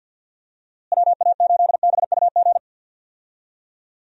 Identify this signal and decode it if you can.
Morse code, heard as a keyed tone, about 47 wpm, 700 Hz. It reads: WA8BUG
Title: USA Call Signs - 6 Characters - Repeated 2x - 50wpm